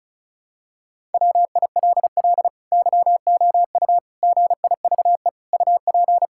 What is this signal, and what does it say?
Morse code, heard as a keyed tone, about 36 wpm, 700 Hz. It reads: WILL YOU GIVE UP
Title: Sentences from Top 1000 Words - Review Part 5 - 35wpm